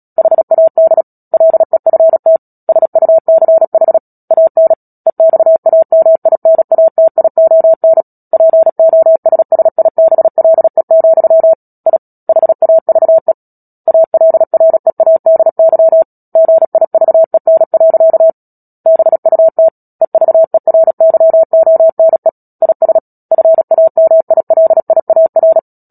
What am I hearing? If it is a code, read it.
Morse code, heard as a keyed tone, about 37 wpm, 650 Hz. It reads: HAD LEFT SUCH AN EXAMINATION POSSIBLE, I HAVE ALREADY GIVEN. BUT EVERYONE IS FAMILIAR
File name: Book Two / 10-Chapter 10 - Epilogue - 36 WPM